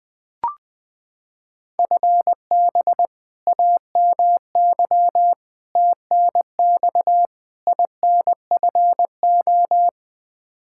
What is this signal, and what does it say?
Morse code, heard as a keyed tone, about 20 wpm, 700 Hz. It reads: FB AMY TNX INFO
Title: Practice QSO - Exchange 3 - More Info - 20wpm